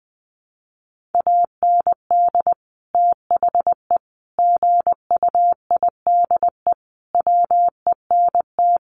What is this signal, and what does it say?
Morse code, heard as a keyed tone, about 20 wpm, 700 Hz. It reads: AND THE GUIDE WENT
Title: Sentences from Top 600 Words 20wpm